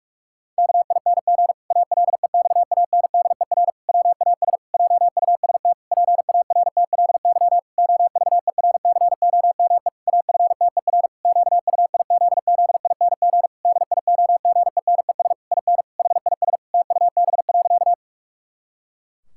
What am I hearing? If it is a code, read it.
Morse code, heard as a keyed tone, about 46 wpm, 700 Hz. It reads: KING ALEXANDER WAS JUST PARTLY OVERCOME AFTER QUIZZING DIOGENES IN HIS TUB.
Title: Pangram Challenge v2 - 45wpm